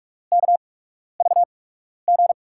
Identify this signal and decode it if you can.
Morse code, heard as a keyed tone, about 44 wpm, spaced as for 12 wpm, 700 Hz. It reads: KVG
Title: USA Call Signs - 5 Characters - 8x Character Spacing - 45wpm